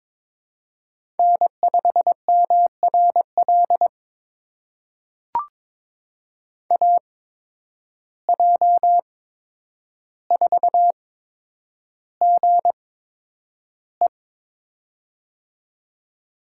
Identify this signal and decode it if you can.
Morse code, heard as a keyed tone, about 22 wpm, spaced as for 9 wpm, 700 Hz. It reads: N5MRL AJ4GE
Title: USA Call Signs - 5 Characters - 8x Character Spacing - 22wpm